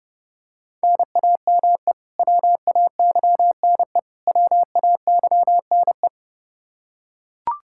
Morse code, heard as a keyed tone, about 30 wpm, 700 Hz. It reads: NAME WAYNE WAYNE
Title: Practice QSO - Exchange 1 - Beginning - 30wpm